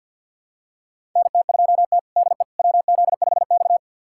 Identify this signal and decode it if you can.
Morse code, heard as a keyed tone, about 51 wpm, 700 Hz. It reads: NT2T DE WZ5X
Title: USA Callsign DE Callsign - 4 Characters - 50wpm